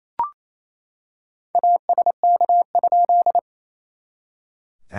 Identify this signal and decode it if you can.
Morse code, heard as a keyed tone, about 28 wpm, 690 Hz. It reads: ASK?
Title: SOTA QSO Practice 28wpm